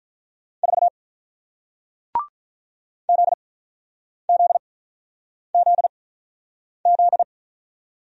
Morse code, heard as a keyed tone, about 48 wpm, spaced as for 9 wpm, 700 Hz. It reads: 4 ZZZZ